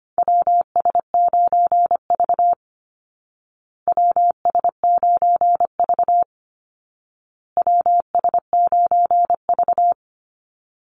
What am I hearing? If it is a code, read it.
Morse code, heard as a keyed tone, about 25 wpm, 700 Hz. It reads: WS9V WS9V WS9V
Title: Call Signs Repeated 2x - 25wpm